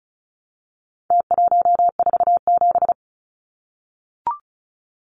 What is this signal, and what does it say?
Morse code, heard as a keyed tone, about 34 wpm, 700 Hz. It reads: T147
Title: Numbers - Contest Style - 4-Digit 35wpm